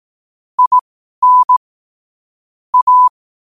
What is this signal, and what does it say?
Morse code, heard as a keyed tone, about 18 wpm, spaced as for 12 wpm, 1000 Hz. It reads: IN A
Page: New quotes every day in morse code at 12 Words per minute.